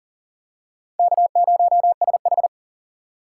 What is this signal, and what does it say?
Morse code, heard as a keyed tone, about 40 wpm, 700 Hz. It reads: K0SH
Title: USA Call Signs - 4 Characters - 4x Character Spacing - 40wpm